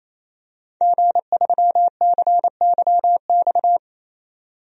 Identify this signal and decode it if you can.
Morse code, heard as a keyed tone, about 28 wpm, 700 Hz. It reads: G3CYX
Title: DXCC-100 Call Signs - 5 Characters - 28wpm